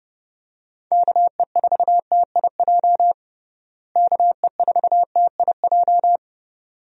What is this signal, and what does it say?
Morse code, heard as a keyed tone, about 30 wpm, 700 Hz. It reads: KE4TIJ KE4TIJ
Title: USA Call Signs - 6 Characters - Repeated 2x - 30wpm